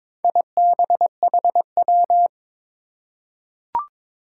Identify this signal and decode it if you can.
Morse code, heard as a keyed tone, about 22 wpm, 700 Hz. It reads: IBHW
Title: Four Challenging Letter-Number Combinations 22wpm